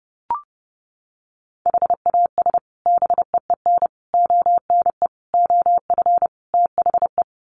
Morse code, heard as a keyed tone, about 30 wpm, 700 Hz. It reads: HAS BEEN ONE OF THE
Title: Sets of 5 Words Encore 30wpm